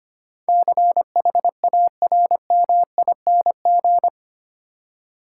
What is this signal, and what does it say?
Morse code, heard as a keyed tone, about 25 wpm, 690 Hz. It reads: CHARMING
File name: 8 Letter Words - First 3 Letters Given - 25wpm